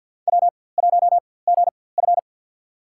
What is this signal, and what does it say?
Morse code, heard as a keyed tone, about 47 wpm, spaced as for 23 wpm, 700 Hz. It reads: W1GF
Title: USA Call Signs - 4x Character Spacing - 50wpm